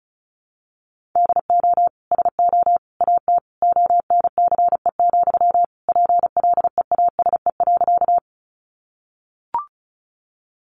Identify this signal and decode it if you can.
Morse code, heard as a keyed tone, about 34 wpm, 700 Hz. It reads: DO SO AT ONCE, PLEASE.
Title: Sentences from Top 1500 Words 35wpm